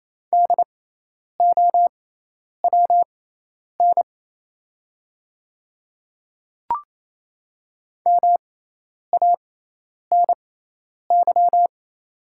Morse code, heard as a keyed tone, about 28 wpm, spaced as for 10 wpm, 700 Hz. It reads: DOWN MANY
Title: 6x Character Spacing - Top 100 Words - 28wpm